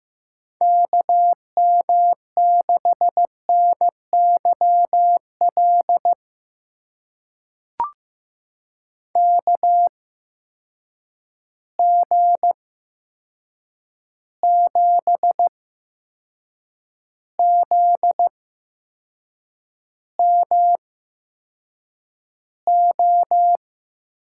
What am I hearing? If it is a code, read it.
Morse code, heard as a keyed tone, about 15 wpm, spaced as for 6 wpm, 700 Hz. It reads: KM6NYL KG7ZMO
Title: USA Call Signs - 6 Characters - 8x Character Spacing - 15wpm